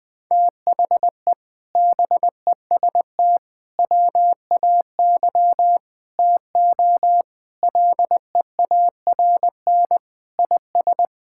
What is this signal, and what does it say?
Morse code, heard as a keyed tone, about 20 wpm, 700 Hz. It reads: THE BEST WAY TO LEARN IS
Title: Sentences with 4-Word Phrases - Spoken Beginning - 20wpm